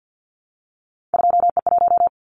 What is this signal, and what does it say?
Morse code, heard as a keyed tone, about 50 wpm, 700 Hz. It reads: 2E1
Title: Callsign Prefixes for Top 25 DXCC - 50wpm